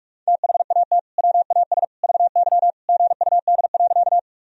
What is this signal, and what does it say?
Morse code, heard as a keyed tone, about 46 wpm, 700 Hz. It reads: THAT WAS VY GUD.
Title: Sentences from Top 200 Words and Abbreviations 45wpm